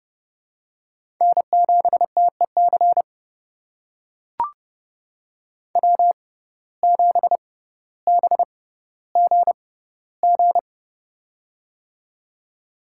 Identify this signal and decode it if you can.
Morse code, heard as a keyed tone, about 32 wpm, spaced as for 16 wpm, 700 Hz. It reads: N7TEC W7BGG
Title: USA Call Signs - 6x Character Spacing - 30wpm